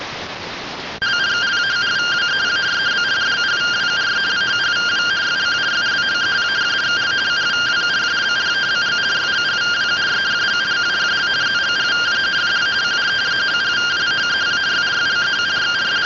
Two-tone teleprinter signal, about 170 Hz shift.